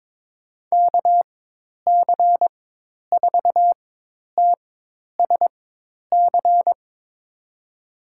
Morse code, heard as a keyed tone, about 22 wpm, spaced as for 10 wpm, 700 Hz. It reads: KC4TSC
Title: USA Call Signs - 6 Characters - 4x Character Spacing - 22wpm